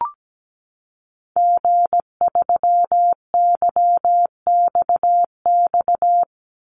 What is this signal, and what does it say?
Morse code, heard as a keyed tone, about 17 wpm, 700 Hz. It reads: G3YXX
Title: Call Signs - 17wpm